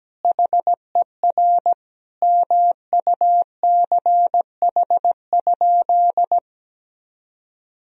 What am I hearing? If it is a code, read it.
Morse code, heard as a keyed tone, about 17 wpm, 700 Hz. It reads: HER MUCH?
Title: Sentences from Top 400 Words and Abbreviations 17wpm